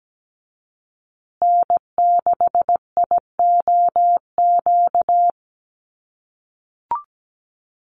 Morse code, heard as a keyed tone, about 17 wpm, 700 Hz. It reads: N6IOQ
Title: USA Call Signs Repeated 3x 17wpm